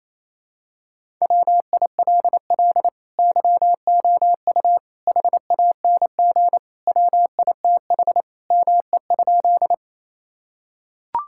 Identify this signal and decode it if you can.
Morse code, heard as a keyed tone, about 28 wpm, 700 Hz. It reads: WILL YOU HANG WITH ME?
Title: Sentences from Top 1600 Words 28wpm